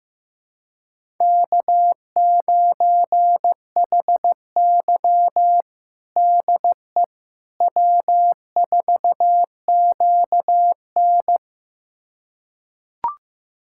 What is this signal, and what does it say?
Morse code, heard as a keyed tone, about 15 wpm, 700 Hz. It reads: K9HY DE W4QN
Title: USA Callsign DE Callsign - 4 Characters - 15wpm